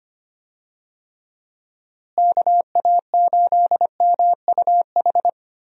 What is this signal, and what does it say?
Morse code, heard as a keyed tone, about 25 wpm, 700 Hz. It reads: KA8MUH
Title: USA Call Signs - 6 Characters - Repeated 3x - 25wpm